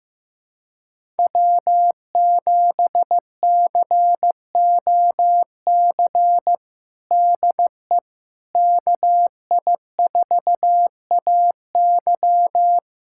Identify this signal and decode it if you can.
Morse code, heard as a keyed tone, about 15 wpm, 700 Hz. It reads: W7COC DE KI4AY
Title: USA Callsign DE Callsign - 5 Characters - 15wpm